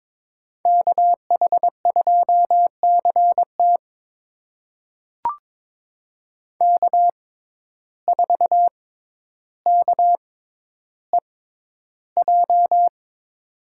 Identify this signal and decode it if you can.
Morse code, heard as a keyed tone, about 22 wpm, spaced as for 11 wpm, 700 Hz. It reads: KH2CT K4KEJ
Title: USA Call Signs - 5 Characters - 6x Character Spacing - 22wpm